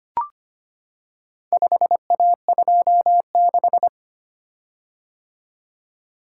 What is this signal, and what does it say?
Morse code, heard as a keyed tone, about 25 wpm, 700 Hz. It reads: 5A26
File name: practice / 25wpm / Four Letter-Number Combinations - 25wpm